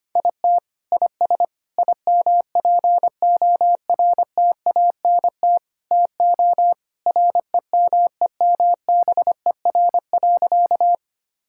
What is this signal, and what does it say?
Morse code, heard as a keyed tone, about 25 wpm, 700 Hz. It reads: IT IS IMPORTANT TO REMEMBER.
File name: practice / 25wpm / Sentences with 4-Word Phrases - 25wpm